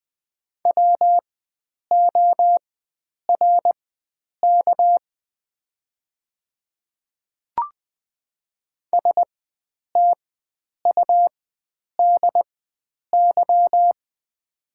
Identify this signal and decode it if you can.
Morse code, heard as a keyed tone, about 20 wpm, spaced as for 9 wpm, 690 Hz. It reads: WORK STUDY
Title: 4x Character Spacing - Top 300 Words - 20wpm